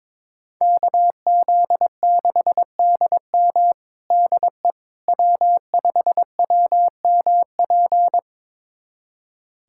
Morse code, heard as a keyed tone, about 21 wpm, 700 Hz. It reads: KZ6DM DE W5WMP
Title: USA Callsign DE Callsign - 5 Characters - 22wpm